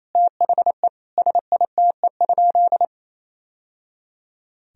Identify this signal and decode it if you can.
Morse code, heard as a keyed tone, about 27 wpm, 700 Hz. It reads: THE SITE?
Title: Sentences from Top 1000 Words - Review Part 4 - 28wpm